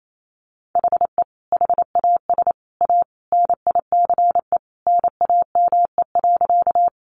Morse code, heard as a keyed tone, about 28 wpm, 700 Hz. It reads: HE HAS A NICE NAME.
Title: Sentences from Top 1100 Words 28wpm